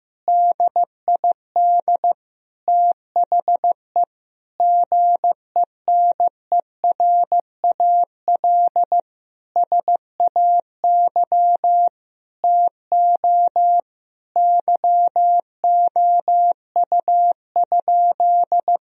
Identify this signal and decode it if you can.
Morse code, heard as a keyed tone, about 15 wpm, 700 Hz. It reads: DID THE GENERAL SAY TO YOU?
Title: Sentences from Top 1000 Words - Review Part 3 - 15wpm